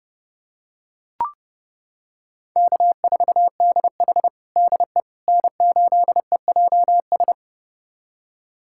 Morse code, heard as a keyed tone, about 32 wpm, 700 Hz. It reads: K4DH DE N8EJS
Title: USA Callsign DE Callsign 30wpm